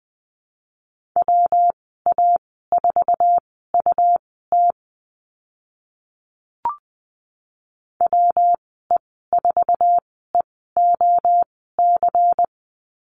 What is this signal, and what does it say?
Morse code, heard as a keyed tone, about 20 wpm, spaced as for 15 wpm, 700 Hz. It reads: WA4UT WE4EOC
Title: 186 Lesson 30g - Letter b - Call Signs 20wpm